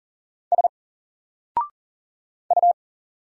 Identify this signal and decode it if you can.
Morse code, heard as a keyed tone, about 40 wpm, 700 Hz. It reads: S U
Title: Single Letter-Number Alt 40wpm